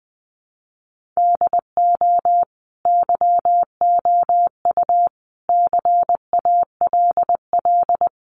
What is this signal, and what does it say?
Morse code, heard as a keyed tone, about 20 wpm, 700 Hz. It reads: DO YOU CALL
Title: Sentences from Top 1500 Words 20wpm